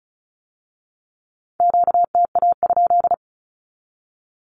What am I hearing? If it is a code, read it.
Morse code, heard as a keyed tone, about 36 wpm, 700 Hz. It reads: QTA?
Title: Q-Codes with Spoken Meaning - 35wpm